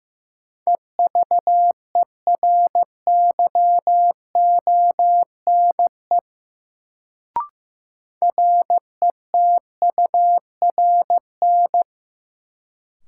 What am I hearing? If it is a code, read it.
Morse code, heard as a keyed tone, about 15 wpm, 700 Hz. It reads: EVERYONE RETURN
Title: Sentences from Top 800 Words - New Word Drill - 15wpm